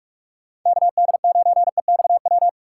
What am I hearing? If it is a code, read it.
Morse code, heard as a keyed tone, about 46 wpm, 700 Hz. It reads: KD9EXW
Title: USA Call Signs - 2x Character Spacing - 45wpm